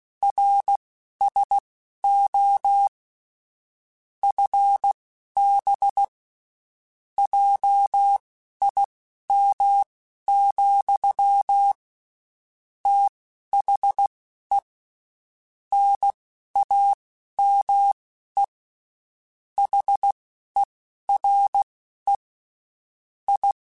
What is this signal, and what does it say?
Morse code, heard as a keyed tone, about 16 wpm, spaced as for 11 wpm, 800 Hz. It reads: RSO FB JIM, THE NAME HERE I
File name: QSOs / QSOs-mp3s / WD0BGP 1@15